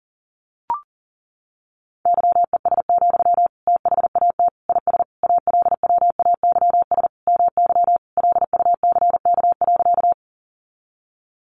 Morse code, heard as a keyed tone, about 40 wpm, 700 Hz. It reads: YES, THAT IS ALWAYS MY LUCK.
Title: Sentences from Top 1500 Words 40wpm